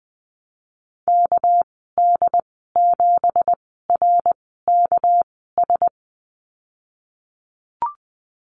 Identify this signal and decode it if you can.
Morse code, heard as a keyed tone, about 20 wpm, 700 Hz. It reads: K D 7 R K S
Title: 238 Lesson 38g - Letter q - Call Signs 20wpm